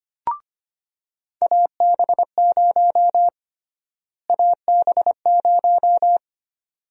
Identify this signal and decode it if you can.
Morse code, heard as a keyed tone, about 25 wpm, 700 Hz. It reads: AB0 AB0
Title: USA Call Sign Prefixes - Repeated 2x - 25wpm